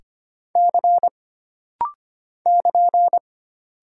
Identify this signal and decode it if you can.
Morse code, heard as a keyed tone, about 25 wpm, 700 Hz. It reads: C <KN>
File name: Single Character (Letter:Number:Punctuation:Prosign) - Rapid-Fire - 25wpm